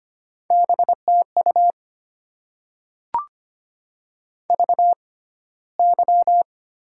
Morse code, heard as a keyed tone, about 25 wpm, spaced as for 15 wpm, 690 Hz. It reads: BTU VY
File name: Top 90 QSO Elements - 6x Character Spacing - 25wpm